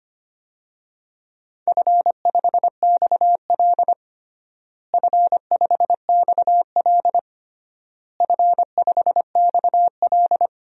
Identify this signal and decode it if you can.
Morse code, heard as a keyed tone, about 25 wpm, 700 Hz. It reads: F5XL F5XL F5XL
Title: DXCC-100 Call Signs - 4 Characters - Repeated 3x - 25wpm